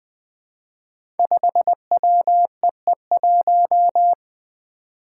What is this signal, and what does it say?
Morse code, heard as a keyed tone, about 20 wpm, 700 Hz. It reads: 5WEE1
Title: Five Challenging Letter-Number Combinations 20wpm